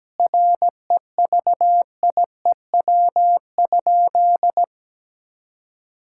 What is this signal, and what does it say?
Morse code, heard as a keyed tone, about 17 wpm, 700 Hz. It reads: REVIEW?
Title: Sentences from Top 1000 Words - Review Part 6 - 17wpm